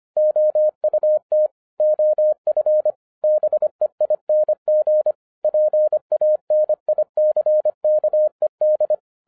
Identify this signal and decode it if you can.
Morse code, heard as a keyed tone, about 25 wpm, 610 Hz. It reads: OUT OF BEING PANICKED
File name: 2011-07-mp3 / n39 THE 3.11 2 25wpm